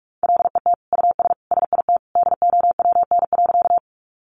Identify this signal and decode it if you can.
Morse code, heard as a keyed tone, about 44 wpm, 700 Hz. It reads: LET US SIT DOWN.
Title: Sentences from Top 500 Words 45wpm